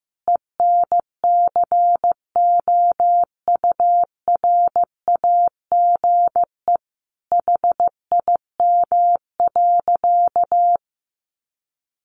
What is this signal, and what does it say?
Morse code, heard as a keyed tone, about 15 wpm, 700 Hz. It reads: ENCOURAGE HIM.
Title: Sentences from Top 1400 Words 15wpm